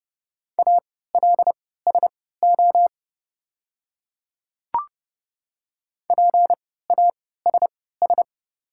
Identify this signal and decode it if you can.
Morse code, heard as a keyed tone, about 30 wpm, spaced as for 17 wpm, 700 Hz. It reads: ALSO PASS
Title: 3x Character Spacing - Top 500 Words - 30wpm